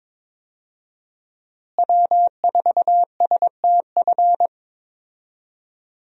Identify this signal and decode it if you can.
Morse code, heard as a keyed tone, about 22 wpm, 700 Hz. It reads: W4STF
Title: USA Call Signs - 5 Characters - Repeated 2x - 22wpm